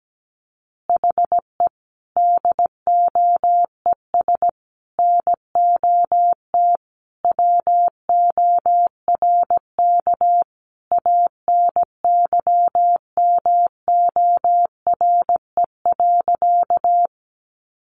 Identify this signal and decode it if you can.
Morse code, heard as a keyed tone, about 17 wpm, 700 Hz. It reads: HE DOES NOT WORK ANYMORE.
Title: Additional Easy Phrases 17wpm